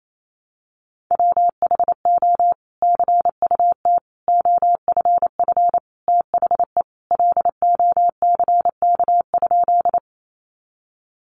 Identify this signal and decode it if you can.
Morse code, heard as a keyed tone, about 28 wpm, 700 Hz. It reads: WHO CUT OFF THE LOCK?
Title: Sentences from Top 1300 Words 28wpm